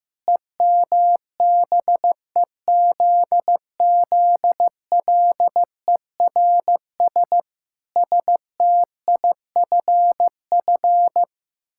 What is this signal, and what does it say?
Morse code, heard as a keyed tone, about 14 wpm, 700 Hz. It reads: EMBEZZLERS STIFF
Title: Pangram Challenge v2 - 15wpm